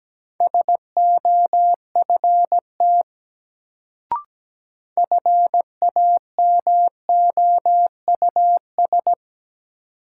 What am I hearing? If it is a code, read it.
Morse code, heard as a keyed tone, about 17 wpm, 700 Hz. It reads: SOFT FAMOUS
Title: Sentences from Top 1500 Words - New Word Drill Alt - 17wpm